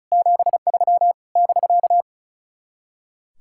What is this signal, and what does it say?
Morse code, heard as a keyed tone, about 42 wpm, 700 Hz. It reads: 73 <BK>
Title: Speed-Racing Common QSO Phrases Encore - 35wpm